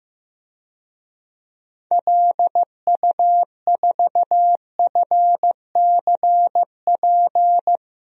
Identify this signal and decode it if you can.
Morse code, heard as a keyed tone, about 15 wpm, 700 Hz. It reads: LU4FCP
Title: DXCC-100 Call Signs - 6 Characters - Repeated 2x - 15wpm